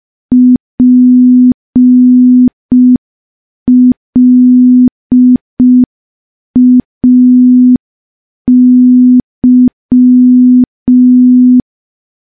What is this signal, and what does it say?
Morse code, heard as a keyed tone, about 5 wpm, 250 Hz. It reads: PLAY
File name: morse / morse-training-05WPM